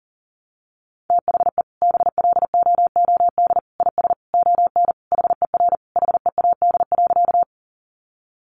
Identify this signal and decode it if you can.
Morse code, heard as a keyed tone, about 40 wpm, 700 Hz. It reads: THE BLOOD IS ON HER HEAD.
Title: Sentences from Top 1100 Words 40wpm